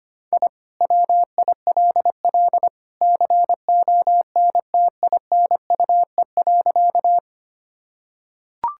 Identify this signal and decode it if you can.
Morse code, heard as a keyed tone, about 25 wpm, 700 Hz. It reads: I WILL CONTINUE.
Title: Sentences from Top 1000 Words - Review Part 6 - 25wpm